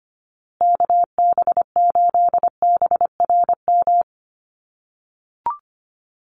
Morse code, heard as a keyed tone, about 25 wpm, 700 Hz. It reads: KB8BRM
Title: USA Call Signs Repeated 2x 25wpm